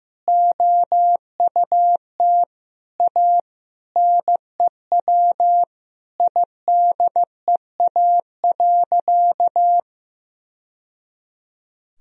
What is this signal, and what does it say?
Morse code, heard as a keyed tone, about 15 wpm, 700 Hz. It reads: OUT A NEW IDEA.
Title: Sentences from Top 1000 Words - Review Part 1 - 15wpm